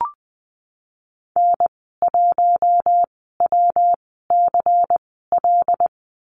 Morse code, heard as a keyed tone, about 20 wpm, spaced as for 15 wpm, 700 Hz. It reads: N1WCL
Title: 122 Lesson 20g - Number 3 - Call Signs 20wpm